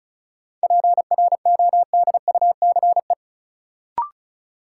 Morse code, heard as a keyed tone, about 36 wpm, 700 Hz. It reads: PRODUCE
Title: Sentences from Top 1000 Words - Review Part 3 - New Word Drill - 35wpm